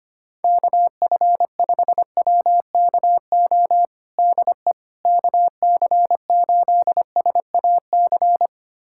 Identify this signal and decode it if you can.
Morse code, heard as a keyed tone, about 25 wpm, 700 Hz. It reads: KF5WKO DE KC8SAC
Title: USA Callsign DE Callsign 25wpm